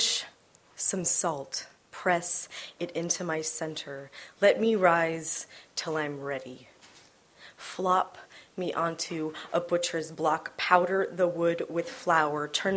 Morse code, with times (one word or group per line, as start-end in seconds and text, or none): none